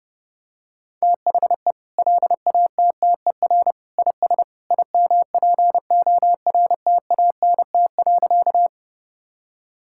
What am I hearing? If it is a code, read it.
Morse code, heard as a keyed tone, about 30 wpm, 700 Hz. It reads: THE LATTER IS IMPORTANT.
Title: Sentences from Top 1600 Words 30wpm